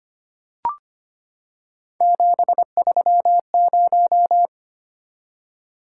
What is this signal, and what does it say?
Morse code, heard as a keyed tone, about 25 wpm, 700 Hz. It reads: 730
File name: Three Letter-Number Combinations - 25wpm